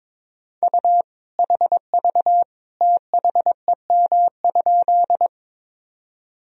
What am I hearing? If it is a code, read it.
Morse code, heard as a keyed tone, about 22 wpm, 700 Hz. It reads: U HV THEM?
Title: Sentences from Top 100 Words and Abbreviations 22wpm